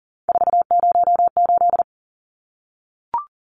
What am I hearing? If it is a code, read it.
Morse code, heard as a keyed tone, about 38 wpm, 700 Hz. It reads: 408
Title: Numbers - 3-Digit 40wpm